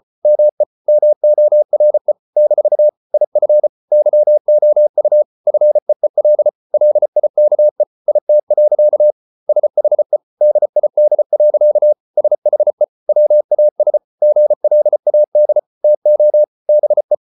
Morse code, heard as a keyed tone, about 34 wpm, 600 Hz. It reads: ME MORE - IF YOU FEEL LIKE IT. SHE DID. SHE WAS GLAD TO BE